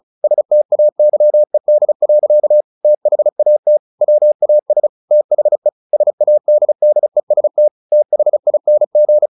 Morse code, heard as a keyed tone, about 34 wpm, 600 Hz. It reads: STAYED. THAT WAS THE SADDEST THING
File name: the old man and the sea / the old man and the sea 1 35wpm